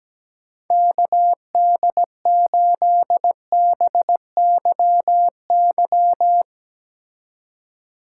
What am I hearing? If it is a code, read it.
Morse code, heard as a keyed tone, about 17 wpm, 700 Hz. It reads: KD8BYY
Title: Speed Racing - USA 2x3 Call Signs 17wpm